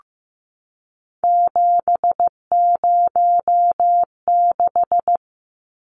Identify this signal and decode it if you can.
Morse code, heard as a keyed tone, about 15 wpm, 700 Hz. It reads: 706
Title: Numbers - 3-Digit 15wpm